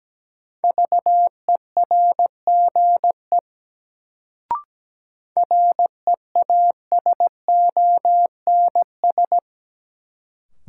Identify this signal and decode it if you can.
Morse code, heard as a keyed tone, about 17 wpm, 700 Hz. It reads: VERGE REASONS
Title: Sets of 4 Words - New Word Drill - 17wpm